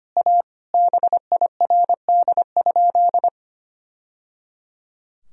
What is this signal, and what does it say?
Morse code, heard as a keyed tone, about 25 wpm, 700 Hz. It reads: A BIRD?
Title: Sentences from Top 1000 Words - Review Part 2 - 25wpm